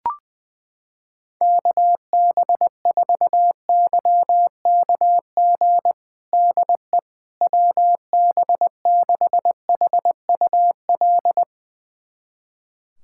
Morse code, heard as a keyed tone, about 21 wpm, 700 Hz. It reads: KB4YKG DE WB6HUL
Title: USA Callsign DE Callsign 20wpm